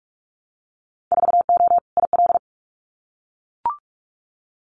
Morse code, heard as a keyed tone, about 44 wpm, 700 Hz. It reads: 4O IL
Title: Winter Field Day Exchanges 45wpm